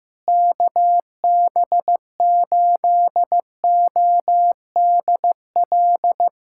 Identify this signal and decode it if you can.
Morse code, heard as a keyed tone, about 15 wpm, 700 Hz. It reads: KB8ODL
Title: USA Callsign DE Callsign 15wpm